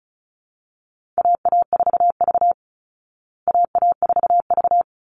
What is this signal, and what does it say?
Morse code, heard as a keyed tone, about 34 wpm, 700 Hz. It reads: AA4V AA4V
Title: Call Signs Repeated 3x - 35wpm